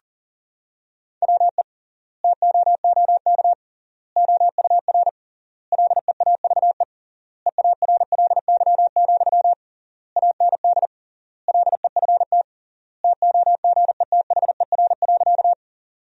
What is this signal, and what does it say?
Morse code, heard as a keyed tone, about 40 wpm, 700 Hz. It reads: WE TOOK OUR LEAVE EARLY, AND LEFT TOGETHER.
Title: 3x Word-Spacing - Sentences from Top 400 Words - 40wpm